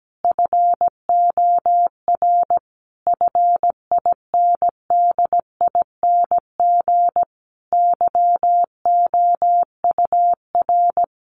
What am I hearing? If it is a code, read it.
Morse code, heard as a keyed tone, about 17 wpm, 700 Hz. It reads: FOR FINDING YOUR
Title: Sentences from Top 1300 Words 17wpm